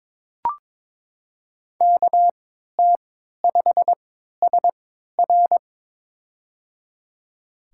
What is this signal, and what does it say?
Morse code, heard as a keyed tone, about 23 wpm, spaced as for 13 wpm, 700 Hz. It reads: KT5SR
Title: USA Call Signs - 3x Character Spacing - 22wpm